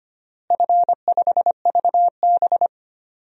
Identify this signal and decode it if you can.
Morse code, heard as a keyed tone, about 25 wpm, 700 Hz. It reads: F5VB
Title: DXCC-100 Call Signs - 4 Characters - Repeated 3x - 25wpm